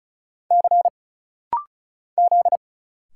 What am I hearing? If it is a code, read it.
Morse code, heard as a keyed tone, about 35 wpm, 700 Hz. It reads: C Z
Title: Single Letter-Number Alt - Rapid-Fire 35wpm